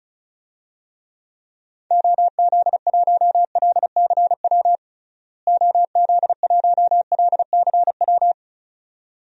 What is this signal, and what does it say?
Morse code, heard as a keyed tone, about 34 wpm, 700 Hz. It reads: OZ1LCW OZ1LCW
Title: DXCC-100 Call Signs - 6 Characters - Repeated 2x - 35wpm